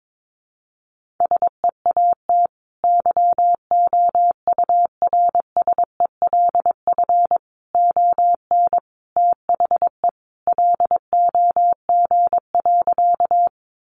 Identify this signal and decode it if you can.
Morse code, heard as a keyed tone, about 22 wpm, 700 Hz. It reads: SEAT YOURSELF ON THE LOG.
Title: Sentences from Top 1500 Words 22wpm